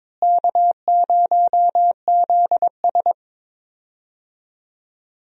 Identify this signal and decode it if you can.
Morse code, heard as a keyed tone, about 22 wpm, 700 Hz. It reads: K0ZS
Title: USA Call Signs - 4 Characters - Repeated 2x - 22wpm